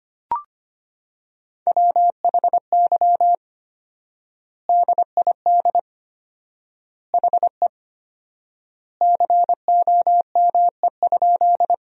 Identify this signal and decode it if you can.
Morse code, heard as a keyed tone, about 25 wpm, 700 Hz. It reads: WHY DID HE COME?
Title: 4x Word-Spacing - Sentences from Top 200 Words - 25wpm